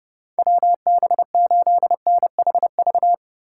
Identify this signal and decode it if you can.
Morse code, heard as a keyed tone, about 30 wpm, 700 Hz. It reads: WB8NHV
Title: USA Call Signs - 3x Character Spacing - 30wpm